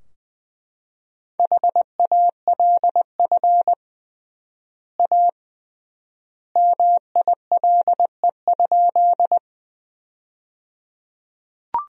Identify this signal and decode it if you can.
Morse code, heard as a keyed tone, about 20 wpm, 700 Hz. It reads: HALF A MILE?
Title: 3x Word-Spacing - Sentences from Top 400 Words - 20wpm